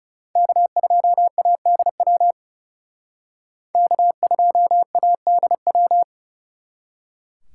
Morse code, heard as a keyed tone, about 38 wpm, 700 Hz. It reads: K2ADW K2ADW